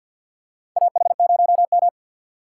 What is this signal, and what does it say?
Morse code, heard as a keyed tone, about 53 wpm, 690 Hz. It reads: AH0M
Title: USA Call Signs - 4 Characters - 6x Character Spacing - 50wpm